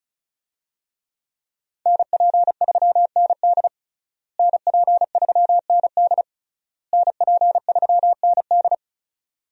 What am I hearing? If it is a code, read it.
Morse code, heard as a keyed tone, about 34 wpm, 700 Hz. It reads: NP3ND NP3ND NP3ND
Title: USA Call Signs - 5 Characters - Repeated 3x - 35wpm